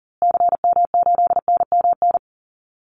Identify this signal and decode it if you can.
Morse code, heard as a keyed tone, about 40 wpm, 700 Hz. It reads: CM8NMN
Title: Call Signs Repeated 2x - 40wpm